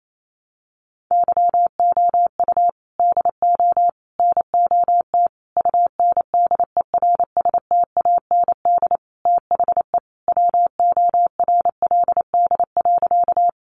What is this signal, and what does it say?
Morse code, heard as a keyed tone, about 28 wpm, 700 Hz. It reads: YOU DO NOT UNDERSTAND THE WORLD.
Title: Sentences from Top 600 Words 28wpm